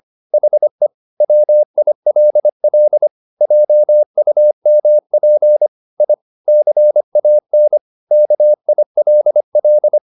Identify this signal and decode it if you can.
Morse code, heard as a keyed tone, about 25 wpm, 600 Hz. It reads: HE WILL JUMP I CAN KILL
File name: the old man and the sea part 1 25wpm